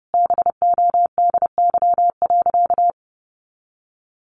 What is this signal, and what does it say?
Morse code, heard as a keyed tone, about 30 wpm, 700 Hz. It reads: BODY.
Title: Sentences from Top 400 Words 30wpm